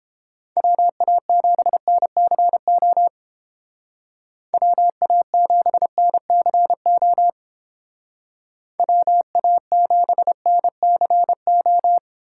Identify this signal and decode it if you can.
Morse code, heard as a keyed tone, about 29 wpm, 700 Hz. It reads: WA7NCO WA7NCO WA7NCO